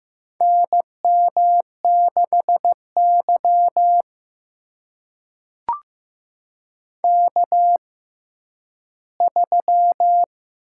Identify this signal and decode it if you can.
Morse code, heard as a keyed tone, about 15 wpm, spaced as for 10 wpm, 700 Hz. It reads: NM6Y K3
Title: USA Call Signs - 6x Character Spacing - 15wpm